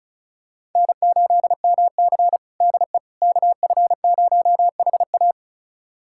Morse code, heard as a keyed tone, about 36 wpm, 700 Hz. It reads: N8MC DE KF0HA
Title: USA Callsign DE Callsign 35wpm